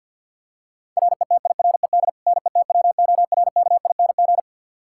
Morse code, heard as a keyed tone, about 47 wpm, 700 Hz. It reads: RETIRED NETWORKING
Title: Practice QSO - Exchange 3 - More Info - 50wpm